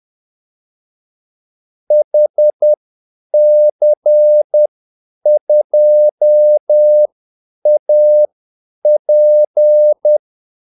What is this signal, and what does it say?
Morse code, heard as a keyed tone, about 10 wpm, spaced as for 8 wpm, 600 Hz. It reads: HC2AP
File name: Callsigns-10-8wpm